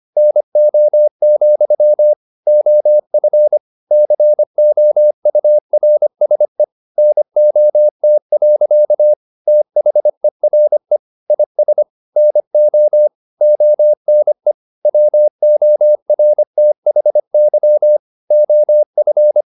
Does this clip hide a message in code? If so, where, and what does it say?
Yes: Morse code from 0.2 to 19.4 s, heard as a keyed tone, about 25 wpm, 600 Hz. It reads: NO, OF COURSE NOT. THERE IS NO ONE WORTHY OF